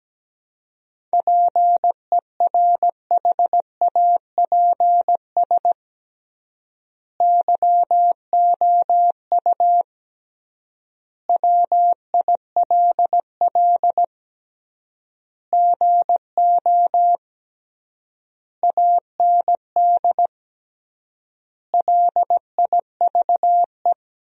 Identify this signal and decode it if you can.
Morse code, heard as a keyed tone, about 17 wpm, 700 Hz. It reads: PERHAPS YOU WILL GO AND LIVE
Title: 3x Word-Spacing - Sentences from Top 500 Words - 17wpm